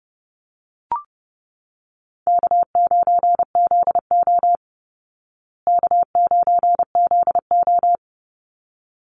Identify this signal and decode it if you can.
Morse code, heard as a keyed tone, about 30 wpm, 700 Hz. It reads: K9ZO K9ZO
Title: Easy Call Signs Repeated 2x - 30wpm